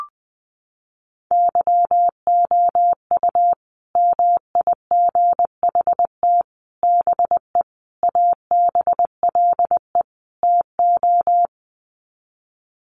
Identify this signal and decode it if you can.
Morse code, heard as a keyed tone, about 20 wpm, 700 Hz. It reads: YOU MIGHT BE ABLE TO
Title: Sets of 5 Words Encore2 20wpm